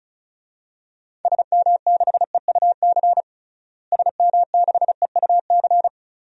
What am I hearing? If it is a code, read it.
Morse code, heard as a keyed tone, about 36 wpm, 700 Hz. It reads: SM6EUC SM6EUC
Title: DXCC-100 Call Signs - 6 Characters - Repeated 2x - 35wpm